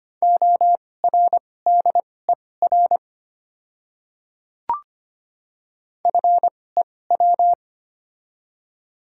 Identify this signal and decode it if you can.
Morse code, heard as a keyed tone, about 25 wpm, spaced as for 18 wpm, 690 Hz. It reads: ORDER FEW
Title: 2x Character Spacing - Top 500 Words - 25wpm